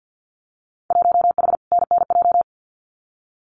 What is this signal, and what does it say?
Morse code, heard as a keyed tone, about 50 wpm, 700 Hz. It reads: 1H NNJ
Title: Winter Field Day Exchanges 50wpm